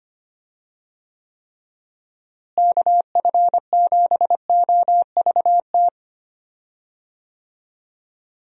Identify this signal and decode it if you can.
Morse code, heard as a keyed tone, about 25 wpm, 700 Hz. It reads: KF7OVT
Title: USA Call Signs - 6 Characters - Repeated 2x - 25wpm